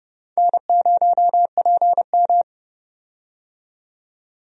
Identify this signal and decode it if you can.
Morse code, heard as a keyed tone, about 30 wpm, 700 Hz. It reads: N0PM
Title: USA Call Signs - 4 Characters - 30wpm